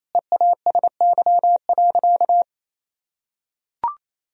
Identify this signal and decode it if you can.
Morse code, heard as a keyed tone, about 28 wpm, 700 Hz. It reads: EASY.
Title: Sentences from Top 1000 Words - Review Part 3 - 28wpm